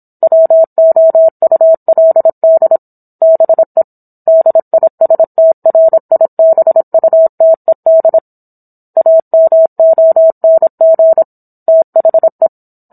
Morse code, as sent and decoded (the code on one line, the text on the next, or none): .-- --- ..- .-.. -.. / -... / -.. .. ... - .-. .. -... ..- - . -.. / .- -- --- -. --. / - .... .
WOULD BE DISTRIBUTED AMONG THE